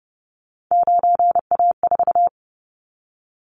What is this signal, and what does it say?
Morse code, heard as a keyed tone, about 30 wpm, 700 Hz. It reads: 9A4
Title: Callsign Prefixes for Top 100 DXCC - 30wpm